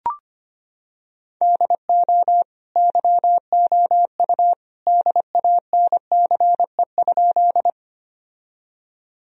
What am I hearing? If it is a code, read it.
Morse code, heard as a keyed tone, about 25 wpm, 700 Hz. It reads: DO YOU DANCE?
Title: Sentences from Top 1000 Words - Review Part 4 - 25wpm